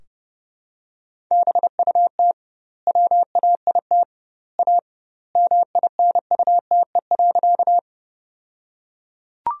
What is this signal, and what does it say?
Morse code, heard as a keyed tone, about 30 wpm, 700 Hz. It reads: BUT WAIT A MINUTE.
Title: 2x Word-Spacing - Sentences from Top 500 Words - 30wpm